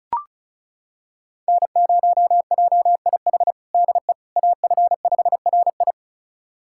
Morse code, heard as a keyed tone, about 36 wpm, 700 Hz. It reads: N0JIH DE AF5RI
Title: USA Callsign DE Callsign - 5 Characters - 35wpm